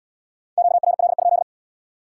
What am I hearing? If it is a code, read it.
Morse code, heard as a keyed tone, about 77 wpm, 700 Hz. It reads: QRL?